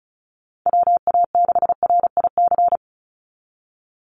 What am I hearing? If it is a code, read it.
Morse code, heard as a keyed tone, about 34 wpm, 700 Hz. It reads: WA6RIC
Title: NA SOTA Chaser Callsigns 35wpm